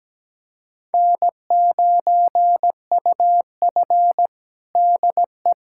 Morse code, heard as a keyed tone, about 17 wpm, 700 Hz. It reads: N9UF DE
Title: USA Callsign DE Callsign - 4 Characters - 17wpm